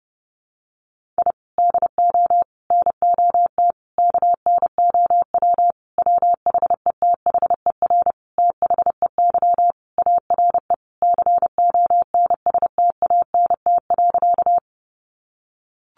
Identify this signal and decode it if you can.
Morse code, heard as a keyed tone, about 30 wpm, 700 Hz. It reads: I DO NOT KNOW WHETHER THEY ARE CONSTANT.
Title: Sentences from Top 1500 Words 30wpm